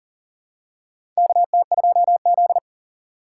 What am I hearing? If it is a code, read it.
Morse code, heard as a keyed tone, about 40 wpm, 700 Hz. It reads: KT2Z
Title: USA Call Signs - 4 Characters - 4x Character Spacing - 40wpm